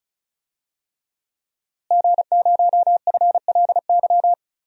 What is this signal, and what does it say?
Morse code, heard as a keyed tone, about 36 wpm, 700 Hz. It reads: G0FLY
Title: DXCC-100 Call Signs - 5 Characters - Repeated 3x - 35wpm